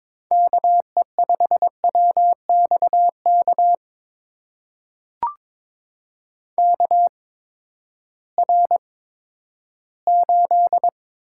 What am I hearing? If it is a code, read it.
Morse code, heard as a keyed tone, about 22 wpm, spaced as for 13 wpm, 700 Hz. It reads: KE5WXK KR8
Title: USA Call Signs - 6 Characters - 8x Character Spacing - 22wpm